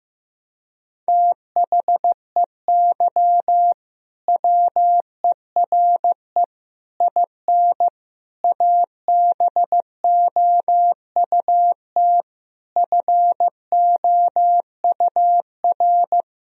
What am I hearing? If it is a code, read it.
Morse code, heard as a keyed tone, about 15 wpm, 700 Hz. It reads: THEY WERE IN ABOUT FOUR
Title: Sentences from Top 1000 Words - Review Part 1 - 15wpm